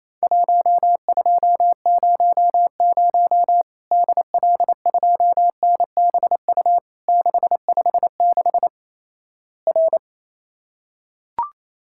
Morse code, heard as a keyed tone, about 27 wpm, 700 Hz. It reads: 1200 DL2NBU 656 R
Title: Worked All Europe DX Contest - QTC Exchange - 28wpm